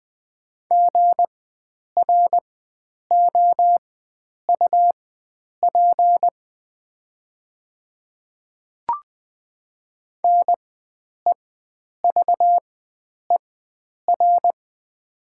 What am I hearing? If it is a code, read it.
Morse code, heard as a keyed tone, about 20 wpm, spaced as for 9 wpm, 700 Hz. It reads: GROUP NEVER
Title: 4x Character Spacing - Top 300 Words - 20wpm